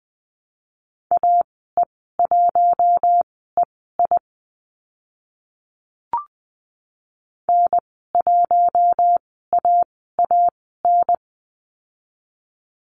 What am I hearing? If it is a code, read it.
Morse code, heard as a keyed tone, about 20 wpm, spaced as for 14 wpm, 700 Hz. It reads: AE1EI N1AAN
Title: 042 Lesson 8g - Number 1 - Call Signs 20wpm